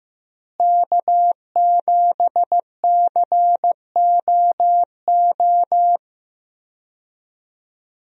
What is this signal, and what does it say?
Morse code, heard as a keyed tone, about 15 wpm, 700 Hz. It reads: K7COO
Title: USA Call Signs - 5 Characters - Repeated 3x - 15wpm